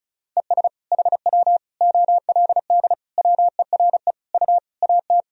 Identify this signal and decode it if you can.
Morse code, heard as a keyed tone, about 34 wpm, 690 Hz. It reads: ES HW OLD WERE U AT
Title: Sentences from Top 200 Words and Abbreviations 35wpm